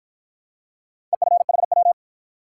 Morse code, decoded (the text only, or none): EFHW